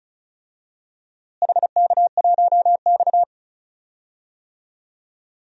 Morse code, heard as a keyed tone, about 36 wpm, 700 Hz. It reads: HK1X
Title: DXCC-100 Call Signs - 4 Characters - Repeated 3x - 35wpm